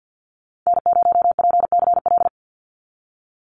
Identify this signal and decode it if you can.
Morse code, heard as a keyed tone, about 50 wpm, 700 Hz. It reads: N0PCL
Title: NA SOTA Activator Callsigns 50wpm